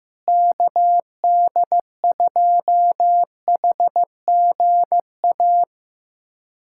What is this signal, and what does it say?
Morse code, heard as a keyed tone, about 15 wpm, 700 Hz. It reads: KD2HGA
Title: USA Callsign DE Callsign 15wpm